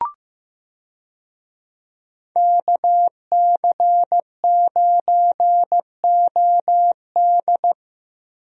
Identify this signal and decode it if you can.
Morse code, heard as a keyed tone, about 15 wpm, 700 Hz. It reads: KC9OD
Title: USA Call Signs - 5 Characters - 15wpm